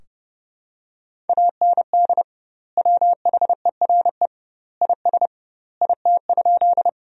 Morse code, heard as a keyed tone, about 29 wpm, 700 Hz. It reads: AND WHERE IS IT?
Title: 2x Word-Spacing - Sentences from Top 200 Words - 30wpm